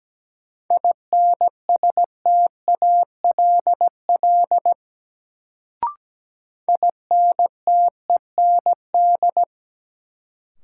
Morse code, heard as a keyed tone, about 17 wpm, 700 Hz. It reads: INSTALL INTEND
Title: Sentences from Top 1600 Words - New Word Drill Alt - 17wpm